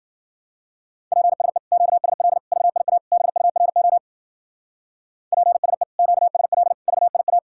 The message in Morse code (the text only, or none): PSE QSL VIA BURO PSE QSL VIA